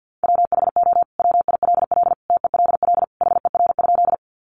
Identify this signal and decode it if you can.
Morse code, heard as a keyed tone, about 50 wpm, 700 Hz. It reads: WHO WILL TELL HER?
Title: Sentences from Top 200 Words 50wpm